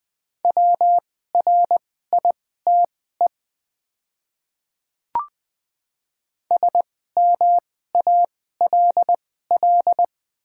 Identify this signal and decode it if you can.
Morse code, heard as a keyed tone, about 20 wpm, spaced as for 15 wpm, 700 Hz. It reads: WRITE SMALL
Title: 2x Character Spacing - Top 200 Words - 20wpm